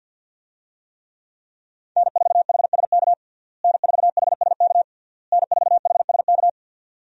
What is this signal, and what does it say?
Morse code, heard as a keyed tone, about 52 wpm, 700 Hz. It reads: N4HSK N4HSK N4HSK
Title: USA Call Signs - 5 Characters - Repeated 3x - 50wpm